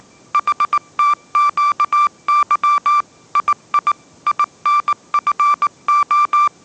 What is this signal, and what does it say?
Morse code, heard as a keyed tone, about 19 wpm, 1170 Hz. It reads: HTQY II INFO
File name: mrx083 20wpm